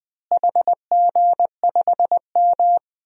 Morse code, heard as a keyed tone, about 21 wpm, 700 Hz. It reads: HG5M
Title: DXCC-100 Call Signs - 4 Characters - Repeated 2x - 20wpm